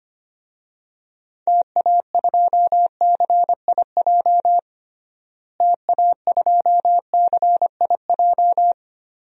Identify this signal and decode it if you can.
Morse code, heard as a keyed tone, about 25 wpm, 700 Hz. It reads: TA2CIJ TA2CIJ
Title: DXCC-100 Call Signs - 6 Characters - Repeated 3x - 25wpm